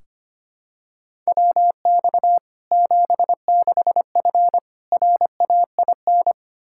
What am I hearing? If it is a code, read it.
Morse code, heard as a keyed tone, about 25 wpm, 700 Hz. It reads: WX 76F RAIN
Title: Practice QSO - Exchange 2 - Info - 25wpm